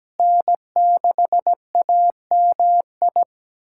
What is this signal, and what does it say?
Morse code, heard as a keyed tone, about 17 wpm, 700 Hz. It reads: N6AMI
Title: USA Callsign DE Callsign 17wpm